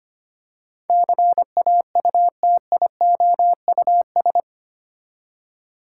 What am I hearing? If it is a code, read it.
Morse code, heard as a keyed tone, about 25 wpm, 700 Hz. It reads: CAUTIOUS
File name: 8 Letter Words - First 3 Letters Given - 25wpm